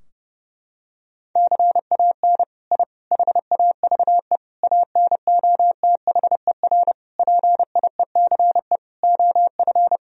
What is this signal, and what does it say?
Morse code, heard as a keyed tone, about 30 wpm, 700 Hz. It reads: CAN I HAVE ANOTHER PIECE OF
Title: Sentences from Top 1000 Words - Review Part 2 - 30wpm